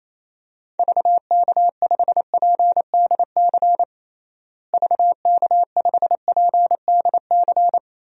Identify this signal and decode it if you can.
Morse code, heard as a keyed tone, about 28 wpm, 700 Hz. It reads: VK5PDC VK5PDC
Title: DXCC-100 Call Signs - 6 Characters - Repeated 2x - 28wpm